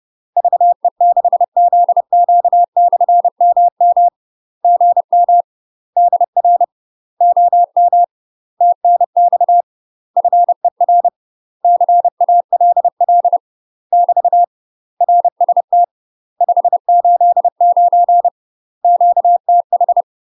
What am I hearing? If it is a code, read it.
Morse code, heard as a keyed tone, about 30 wpm, 700 Hz. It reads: VE6ZQ/MM GM DR OM TNX FER CALL <BT> RST 589 QTH
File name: UK MASTER QSOs TRAINING/30WPM